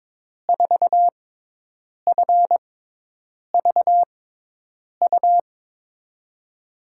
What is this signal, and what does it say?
Morse code, heard as a keyed tone, about 22 wpm, spaced as for 8 wpm, 700 Hz. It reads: 4FVU
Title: USA Call Signs - 6 Characters - 6x Character Spacing - 22wpm